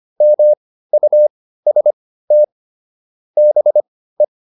Morse code, heard as a keyed tone, about 25 wpm, spaced as for 15 wpm, 600 Hz. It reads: MUST BE